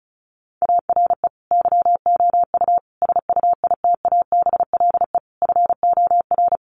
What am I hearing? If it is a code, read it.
Morse code, heard as a keyed tone, about 34 wpm, 700 Hz. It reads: ARE YOU SUITABLE FOR
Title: Sentences from Top 1400 Words 35wpm